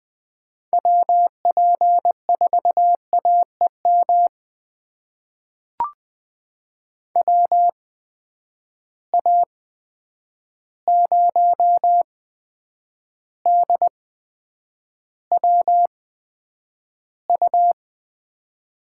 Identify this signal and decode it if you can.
Morse code, heard as a keyed tone, about 20 wpm, spaced as for 9 wpm, 700 Hz. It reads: WP4AEM WA0DWU
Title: USA Call Signs - 6 Characters - 8x Character Spacing - 20wpm